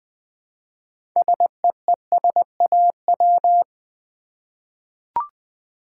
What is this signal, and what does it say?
Morse code, heard as a keyed tone, about 20 wpm, 700 Hz. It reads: SEESAW
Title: Easy Words with Hints - 20wpm